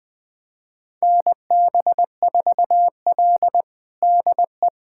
Morse code, heard as a keyed tone, about 21 wpm, 700 Hz. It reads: NB4L DE
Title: USA Callsign DE Callsign - 4 Characters - 20wpm